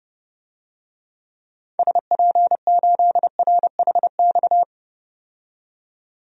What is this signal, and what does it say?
Morse code, heard as a keyed tone, about 31 wpm, 700 Hz. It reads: SP8RHX
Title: DXCC-100 Call Signs - 6 Characters - 30wpm